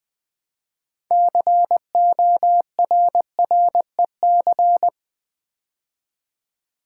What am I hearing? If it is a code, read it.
Morse code, heard as a keyed tone, about 20 wpm, 700 Hz. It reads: CORREC
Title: Build A Word - 3 Letters At A Time - 9-Letter Words - 20wpm